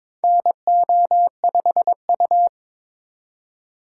Morse code, heard as a keyed tone, about 23 wpm, 700 Hz. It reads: NO5U
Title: USA Call Signs - 4 Characters - 4x Character Spacing - 22wpm